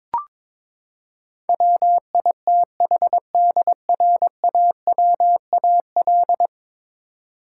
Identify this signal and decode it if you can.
Morse code, heard as a keyed tone, about 22 wpm, 700 Hz. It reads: WITHDRAWAL
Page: morse-code-ninja-practice-22wpm